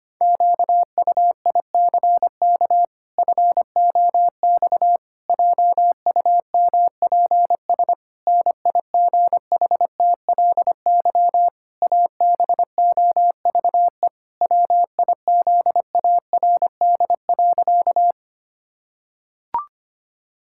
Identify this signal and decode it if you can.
Morse code, heard as a keyed tone, about 25 wpm, 700 Hz. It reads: QUICK FOX JUMPS NIGHTLY ABOVE WIZARD.
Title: Pangram Challenge v2 - 25wpm